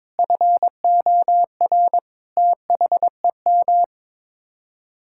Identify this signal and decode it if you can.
Morse code, heard as a keyed tone, about 22 wpm, 700 Hz. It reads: FOR THEM
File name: practice / 22wpm / Sets of 7 Words - 22wpm